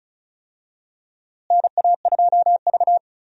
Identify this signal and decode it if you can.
Morse code, heard as a keyed tone, about 34 wpm, 700 Hz. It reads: NA2V
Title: USA Call Signs - 4 Characters - Repeated 3x - 35wpm